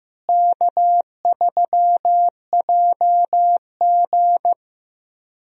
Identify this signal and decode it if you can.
Morse code, heard as a keyed tone, about 15 wpm, 700 Hz. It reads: K3JG
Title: USA Call Signs - 4 Characters - Repeated 3x - 15wpm